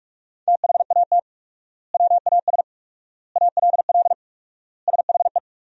Morse code, heard as a keyed tone, about 41 wpm, 700 Hz. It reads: THAT WAS ALL SHE
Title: 4x Word-Spacing - Sentences from Top 100 Words - 45wpm